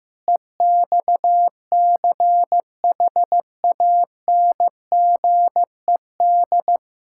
Morse code, heard as a keyed tone, about 15 wpm, 700 Hz. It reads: EXCHANGED
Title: Pangram Challenge v2 - 15wpm